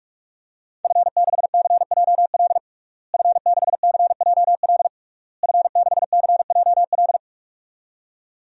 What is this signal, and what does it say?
Morse code, heard as a keyed tone, about 43 wpm, 700 Hz. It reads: F6CJL F6CJL F6CJL
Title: DXCC-100 Call Signs - 5 Characters - Repeated 3x - 45wpm